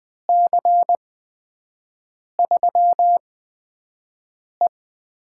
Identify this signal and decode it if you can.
Morse code, heard as a keyed tone, about 21 wpm, spaced as for 5 wpm, 700 Hz. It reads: C3E
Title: USA Call Signs - 6 Characters - 8x Character Spacing - 20wpm